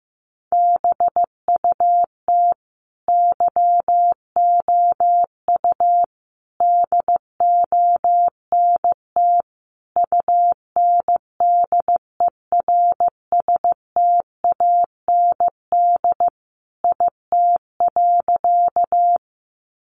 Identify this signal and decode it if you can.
Morse code, heard as a keyed tone, about 15 wpm, 700 Hz. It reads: BUT YOU DONT UNDERSTAND IT.
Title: Sentences from Top 600 Words 15wpm